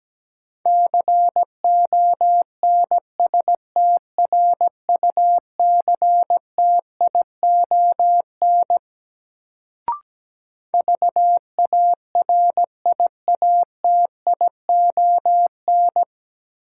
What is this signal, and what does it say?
Morse code, heard as a keyed tone, about 17 wpm, 700 Hz. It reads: CONSTRUCTION VARIATION
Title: Sentences from Top 1600 Words - New Word Drill - 17wpm